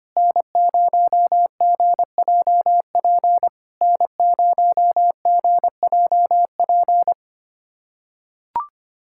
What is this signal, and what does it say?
Morse code, heard as a keyed tone, about 25 wpm, 690 Hz. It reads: N0GJP N0GJP
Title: SOTA QSO Practice 25wpm